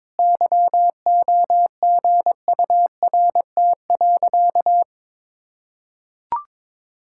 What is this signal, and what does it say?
Morse code, heard as a keyed tone, about 22 wpm, 700 Hz. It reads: YOGURT.
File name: practice / 22wpm / Easy Sentences with Hints - 22wpm